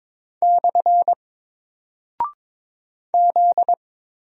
Morse code, heard as a keyed tone, about 22 wpm, 700 Hz. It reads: / Z
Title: Single Character (Letter:Number:Punctuation:Prosign) 22wpm